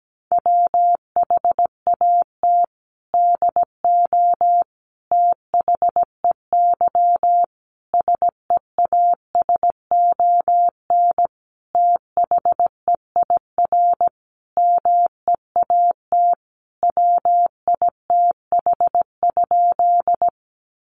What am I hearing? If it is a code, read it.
Morse code, heard as a keyed tone, about 17 wpm, 700 Hz. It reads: WHAT DO THEY SEASON THEIR MEAT WITH?
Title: Sentences from Top 1400 Words 17wpm